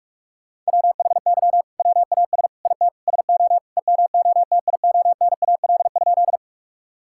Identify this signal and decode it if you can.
Morse code, heard as a keyed tone, about 46 wpm, 700 Hz. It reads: WHY WAS IT SO EMOTIONAL?
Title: Sentences from Top 1000 Words - Review Part 5 - 45wpm